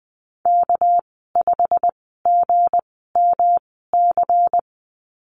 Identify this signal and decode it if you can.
Morse code, heard as a keyed tone, about 20 wpm, spaced as for 14 wpm, 700 Hz. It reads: K5GMC
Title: 245 Lesson 39g - Letter z - Call Signs 20wpm